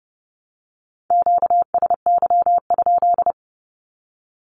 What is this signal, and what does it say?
Morse code, heard as a keyed tone, about 30 wpm, 700 Hz. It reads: QSY?
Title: Q-Codes 30wpm